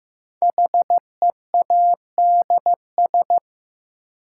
Morse code, heard as a keyed tone, about 15 wpm, 700 Hz. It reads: HEADS
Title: Sentences from Top 1000 Words - Review Part 3 - New Word Drill - 15wpm